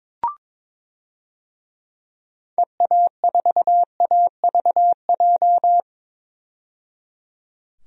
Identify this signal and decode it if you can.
Morse code, heard as a keyed tone, about 21 wpm, 700 Hz. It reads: EA4AVJ
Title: DXCC-100 Call Signs - 6 Characters - 22wpm